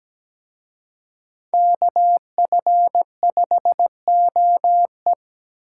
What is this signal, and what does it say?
Morse code, heard as a keyed tone, about 20 wpm, 700 Hz. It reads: KF5OE
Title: Speed Racing - USA 2x2 Call Signs 17wpm